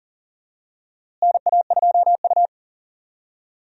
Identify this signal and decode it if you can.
Morse code, heard as a keyed tone, about 39 wpm, 700 Hz. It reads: NA2U
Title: USA Call Signs - 4 Characters - Repeated 2x - 40wpm